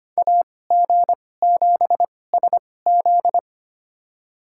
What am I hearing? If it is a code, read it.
Morse code, heard as a keyed tone, about 25 wpm, spaced as for 18 wpm, 700 Hz. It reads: AG7SZ
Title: USA Call Signs - 5 Characters - 2x Character Spacing - 25wpm